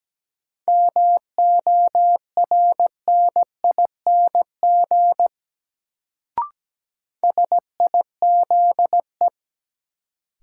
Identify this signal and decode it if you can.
Morse code, heard as a keyed tone, about 17 wpm, 700 Hz. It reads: MORNING SIZE
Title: Sets of 3 Words - New Word Drill - 17wpm